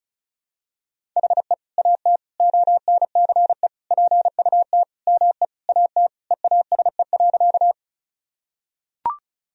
Morse code, heard as a keyed tone, about 36 wpm, 700 Hz. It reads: HE AT ONCE PUT ME AT EASE.
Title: Sentences from Top 1000 Words - Review Part 1 - 35wpm